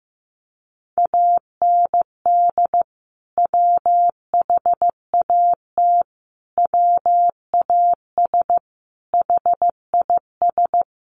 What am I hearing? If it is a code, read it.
Morse code, heard as a keyed tone, about 15 wpm, 700 Hz. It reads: AND WHAT WAS HIS
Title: Sentences from Top 200 Words 15wpm